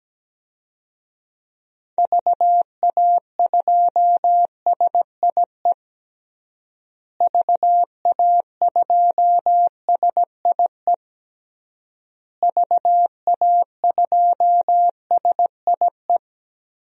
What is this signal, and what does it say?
Morse code, heard as a keyed tone, about 17 wpm, 700 Hz. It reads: VA2SIE VA2SIE VA2SIE
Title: DXCC-100 Call Signs - 6 Characters - Repeated 3x - 17wpm